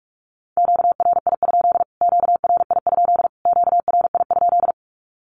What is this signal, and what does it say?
Morse code, heard as a keyed tone, about 44 wpm, 700 Hz. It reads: QRI? QRI? QRI?
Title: Q-Codes Repeated 3x 45wpm